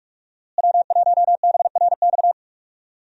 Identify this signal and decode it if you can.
Morse code, heard as a keyed tone, about 45 wpm, 700 Hz. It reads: W1BRX
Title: USA Call Signs - 5 Characters - Repeated 3x - 45wpm